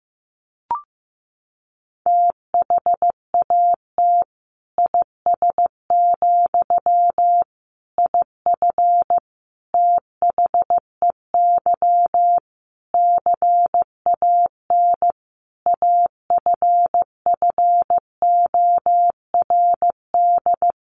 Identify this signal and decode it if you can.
Morse code, heard as a keyed tone, about 15 wpm, 700 Hz. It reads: THAT IS, IF THEY CAN AFFORD
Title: Sentences from Top 1100 Words 15wpm